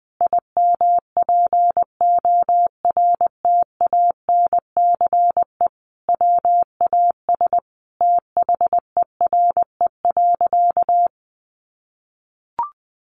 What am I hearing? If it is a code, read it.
Morse code, heard as a keyed tone, about 19 wpm, 700 Hz. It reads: IMPORTANCE WAS THERE.
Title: Sentences from Top 1400 Words 20wpm